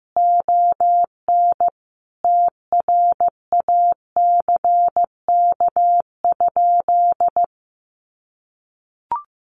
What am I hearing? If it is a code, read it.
Morse code, heard as a keyed tone, about 15 wpm, 700 Hz. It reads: ON TRACK?
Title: Sentences from Top 1200 Words 15wpm